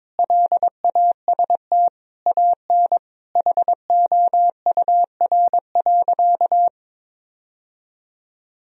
Morse code, heard as a keyed tone, about 22 wpm, 700 Hz. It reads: LAST AN HOUR.
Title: Sentences from Top 1000 Words - Review Part 1 - 22wpm